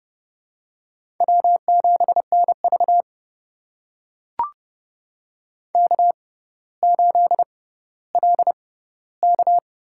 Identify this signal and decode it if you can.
Morse code, heard as a keyed tone, about 32 wpm, spaced as for 16 wpm, 700 Hz. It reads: W7NV K8LK
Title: USA Call Signs - 6x Character Spacing - 30wpm